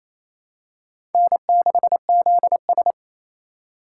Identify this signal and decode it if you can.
Morse code, heard as a keyed tone, about 28 wpm, 700 Hz. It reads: N6ZS
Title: USA Call Signs - 4 Characters - 4x Character Spacing - 28wpm